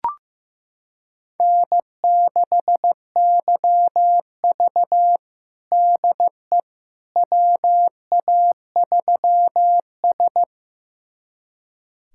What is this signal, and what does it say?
Morse code, heard as a keyed tone, about 15 wpm, 700 Hz. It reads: N6YV DE WA3S
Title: USA Callsign DE Callsign - 4 Characters - 15wpm